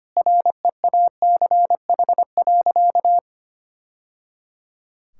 Morse code, heard as a keyed tone, about 25 wpm, 700 Hz. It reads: REACH.
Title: Sentences from Top 1000 Words - Review Part 6 - 25wpm